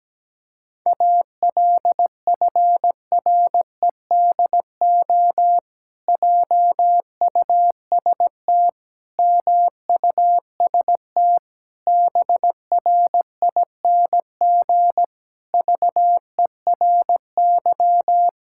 Morse code, heard as a keyed tone, about 17 wpm, 700 Hz. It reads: ALFREDO JUST MUST BRING VERY
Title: Pangram Challenge v2 - 17wpm